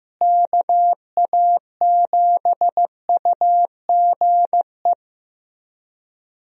Morse code, heard as a keyed tone, about 15 wpm, 700 Hz. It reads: KA7UGE
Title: USA Call Signs - 6 Characters - Repeated 3x - 15wpm